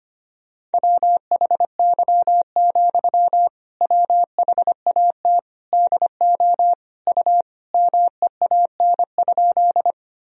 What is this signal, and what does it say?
Morse code, heard as a keyed tone, about 25 wpm, 700 Hz. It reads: WHY, WHAT DO U MEAN?
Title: Sentences from Top 200 Words and Abbreviations 25wpm